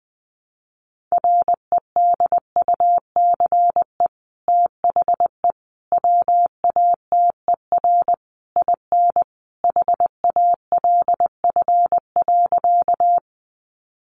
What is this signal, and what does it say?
Morse code, heard as a keyed tone, about 20 wpm, 700 Hz. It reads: REDUCE THE WATER IN HALF.
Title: Sentences from Top 1100 Words 20wpm